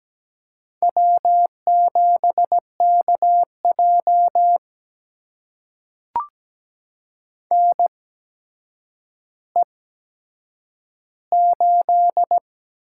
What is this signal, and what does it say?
Morse code, heard as a keyed tone, about 17 wpm, spaced as for 8 wpm, 700 Hz. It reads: W7KJ NE8
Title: USA Call Signs - 4 Characters - 8x Character Spacing - 17wpm